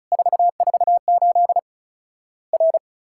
Morse code, heard as a keyed tone, about 36 wpm, 700 Hz. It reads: 448 R
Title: Worked All Europe DX Contest - QTC Exchange - 35wpm